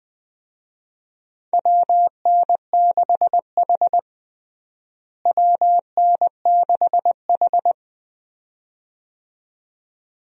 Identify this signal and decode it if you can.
Morse code, heard as a keyed tone, about 20 wpm, 700 Hz. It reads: WN6H WN6H
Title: USA Call Signs - 4 Characters - Repeated 2x - 20wpm